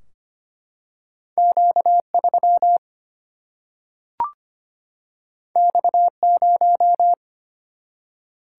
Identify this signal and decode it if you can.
Morse code, heard as a keyed tone, about 25 wpm, 700 Hz. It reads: Q3 X0
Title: Single-Letter Single-Number 25wpm